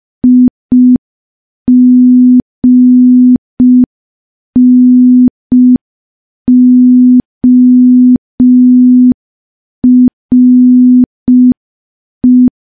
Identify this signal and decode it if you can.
Morse code, heard as a keyed tone, about 5 wpm, 250 Hz. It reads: IGNORE
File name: morse / morse-training-05WPM